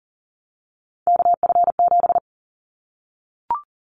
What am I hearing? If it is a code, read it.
Morse code, heard as a keyed tone, about 40 wpm, 700 Hz. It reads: KF7
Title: Callsign Prefixes for Top 25 DXCC - 40wpm